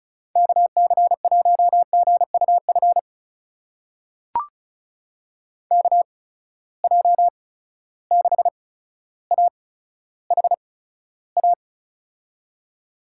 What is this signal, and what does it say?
Morse code, heard as a keyed tone, about 34 wpm, spaced as for 15 wpm, 700 Hz. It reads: KC1GUF KJ6AHA
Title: USA Call Signs - 6 Characters - 8x Character Spacing - 35wpm